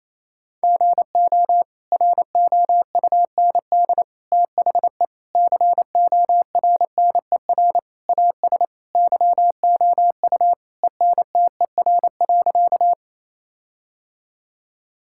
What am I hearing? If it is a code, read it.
Morse code, heard as a keyed tone, about 28 wpm, 700 Hz. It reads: GO ROUND THE CORNER AS YOU ENTER.
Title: Sentences from Top 1600 Words 28wpm